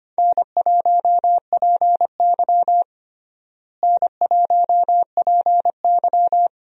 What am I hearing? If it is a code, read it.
Morse code, heard as a keyed tone, about 25 wpm, 700 Hz. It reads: N1PY N1PY
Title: USA Call Signs - 4 Characters - Repeated 2x - 25wpm